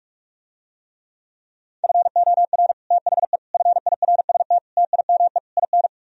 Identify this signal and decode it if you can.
Morse code, heard as a keyed tone, about 45 wpm, 700 Hz. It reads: FOR THE FIRST TIME IN